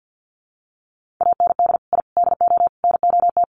Morse code, heard as a keyed tone, about 48 wpm, 700 Hz. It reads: AND I DO NOT
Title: Sets of 4 Words 50wpm